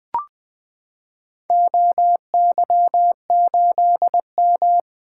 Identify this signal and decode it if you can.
Morse code, heard as a keyed tone, about 21 wpm, 700 Hz. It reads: OY8M
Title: Four Letter-Number Combinations - 20wpm